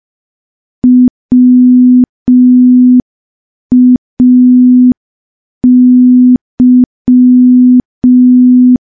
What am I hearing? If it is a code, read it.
Morse code, heard as a keyed tone, about 5 wpm, 250 Hz. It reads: WAY
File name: morse / morse-training-05WPM